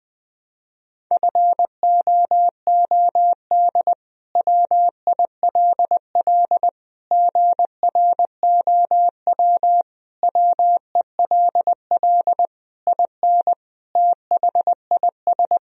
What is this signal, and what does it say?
Morse code, heard as a keyed tone, about 21 wpm, 700 Hz. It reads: FOOD WILL GROW WELL IN THIS
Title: Sentences from Top 1000 Words - Review Part 2 - 20wpm